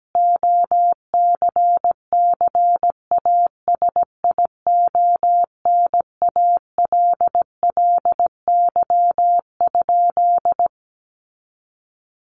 Morse code, heard as a keyed tone, about 17 wpm, 700 Hz. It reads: OCCASIONALLY?
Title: Sentences from Top 1300 Words 17wpm